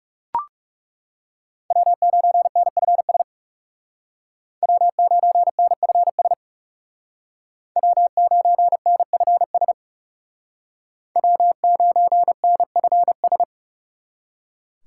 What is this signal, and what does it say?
Morse code, heard as a keyed tone, about 36 wpm, 700 Hz. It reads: W9NFS W9NFS W9NFS W9NFS
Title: Speed Racing - USA 1x3 Call Signs 30wpm